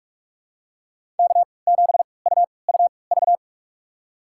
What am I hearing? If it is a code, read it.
Morse code, heard as a keyed tone, about 46 wpm, spaced as for 26 wpm, 700 Hz. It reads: K7UUV
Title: USA Call Signs - 5 Characters - 3x Character Spacing - 45wpm